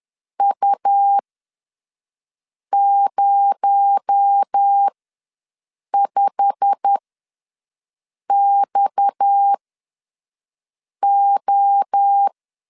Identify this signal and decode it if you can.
Morse code, heard as a keyed tone, about 10 wpm, spaced as for 5 wpm, 800 Hz. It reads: U05XO
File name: LEZIONI / 8 YJ45 / 8 YJ45 ws 10wpm 3843